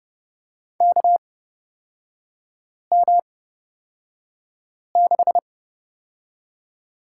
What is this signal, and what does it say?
Morse code, heard as a keyed tone, about 30 wpm, spaced as for 5 wpm, 700 Hz. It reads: KM6
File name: Lesson 36e - BT - Three Characters 30wpm